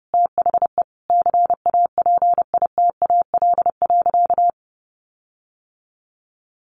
Morse code, heard as a keyed tone, about 30 wpm, 700 Hz. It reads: THE CAPITAL.
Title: Sentences from Top 1500 Words 30wpm